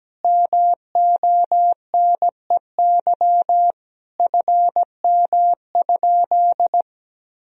Morse code, heard as a keyed tone, about 17 wpm, 700 Hz. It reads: MONEY FM?
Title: Sentences from Top 400 Words and Abbreviations 17wpm